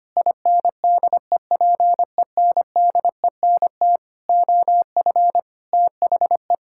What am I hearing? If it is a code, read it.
Morse code, heard as a keyed tone, about 25 wpm, 700 Hz. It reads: INDEPENDENT OF THE
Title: Sentences from Top 1000 Words - Review Part 6 - 25wpm